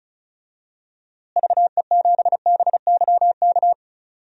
Morse code, heard as a keyed tone, about 36 wpm, 700 Hz. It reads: VE7BYK
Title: DXCC-100 Call Signs - 6 Characters - Repeated 2x - 35wpm